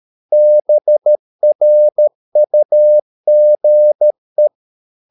Morse code, heard as a keyed tone, about 13 wpm, 600 Hz. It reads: BRUGE
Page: Ferie 13 wpm